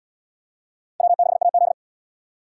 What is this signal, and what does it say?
Morse code, heard as a keyed tone, about 77 wpm, 700 Hz. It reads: W6AZ